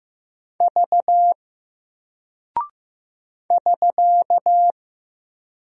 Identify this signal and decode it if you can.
Morse code, heard as a keyed tone, about 15 wpm, 700 Hz. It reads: V <SK>
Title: Single Character (Letter:Number:Punctuation:Prosign) Alt 15wpm